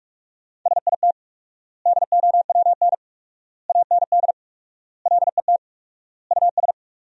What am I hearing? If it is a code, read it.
Morse code, heard as a keyed tone, about 46 wpm, 700 Hz. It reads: SIT DOWN AND LET US
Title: 4x Word-Spacing - Sentences from Top 500 Words - 45wpm